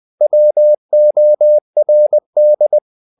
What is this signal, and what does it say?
Morse code, heard as a keyed tone, about 20 wpm, 600 Hz. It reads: WORD
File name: CW / Practice / If20